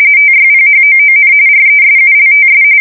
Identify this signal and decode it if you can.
RTTY, about 45.45 baud. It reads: TYPHOON RESULTS
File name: fsk170 / battleshipnj